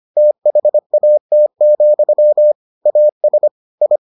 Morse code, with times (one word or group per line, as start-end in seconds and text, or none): 0.2-2.5 THAT,
2.9-3.5 AS
3.8-4.0 I